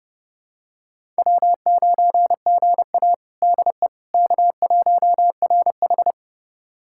Morse code, heard as a keyed tone, about 30 wpm, 700 Hz. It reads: W9GA DE K1RH
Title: USA Callsign DE Callsign - 4 Characters - 30wpm